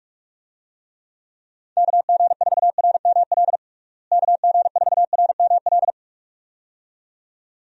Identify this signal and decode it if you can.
Morse code, heard as a keyed tone, about 46 wpm, 700 Hz. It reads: KG4RML KG4RML
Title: USA Call Signs - 6 Characters - Repeated 2x - 45wpm